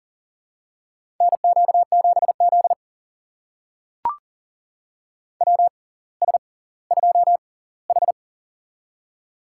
Morse code, heard as a keyed tone, about 40 wpm, spaced as for 20 wpm, 700 Hz. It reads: NQ7Z WS2H
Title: USA Call Signs - 6x Character Spacing - 40wpm